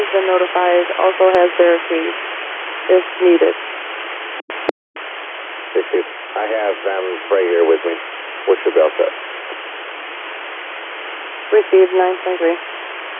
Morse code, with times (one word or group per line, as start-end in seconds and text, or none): none